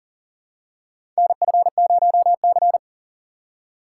Morse code, heard as a keyed tone, about 40 wpm, 700 Hz. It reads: NF0C
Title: USA Call Signs - 4 Characters - 3x Character Spacing - 40wpm